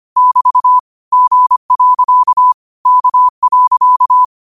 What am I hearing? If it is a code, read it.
Morse code, heard as a keyed tone, about 25 wpm, 1000 Hz. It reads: <BT> G. K.
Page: New quotes every day in morse code at 25 Words per minute.